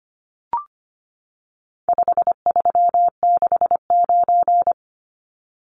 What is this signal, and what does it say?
Morse code, heard as a keyed tone, about 25 wpm, 700 Hz. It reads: 5369
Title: Numbers - 4-Digit 25wpm